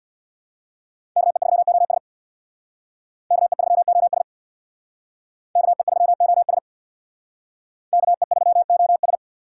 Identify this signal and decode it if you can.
Morse code, heard as a keyed tone, about 62 wpm, 700 Hz. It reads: KE3OS KE3OS KE3OS KE3OS
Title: Speed Racing - USA 2x2 Call Signs 50wpm